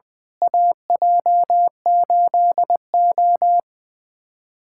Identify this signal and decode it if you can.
Morse code, heard as a keyed tone, about 20 wpm, 700 Hz. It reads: AJ8O
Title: USA Callsign DE Callsign - 4 Characters - 20wpm